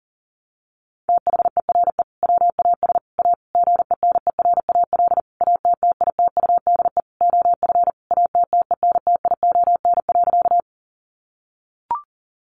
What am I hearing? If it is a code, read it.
Morse code, heard as a keyed tone, about 40 wpm, 700 Hz. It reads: THERE WAS A GENERAL ATTITUDE OF ATTENTION.
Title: Sentences from Top 1300 Words 40wpm